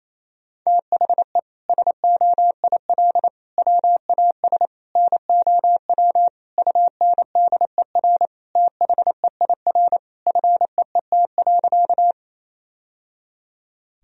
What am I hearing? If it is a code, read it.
Morse code, heard as a keyed tone, about 28 wpm, 700 Hz. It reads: THE SOIL WAS NOW UNDER THEIR FEET.
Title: Sentences from Top 1000 Words - Review Part 4 - 28wpm